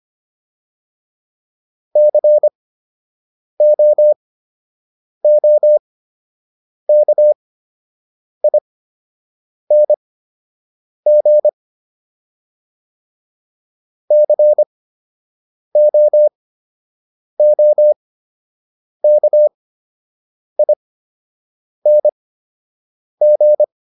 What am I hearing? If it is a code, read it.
Morse code, heard as a keyed tone, about 25 wpm, spaced as for 7 wpm, 600 Hz. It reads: COOKING COOKING